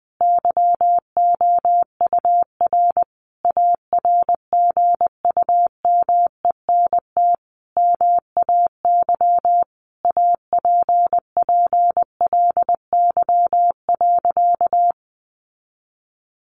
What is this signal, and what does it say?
Morse code, heard as a keyed tone, about 20 wpm, 700 Hz. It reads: YOUR ARGUMENT MAY APPLY.
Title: Sentences from Top 1400 Words 20wpm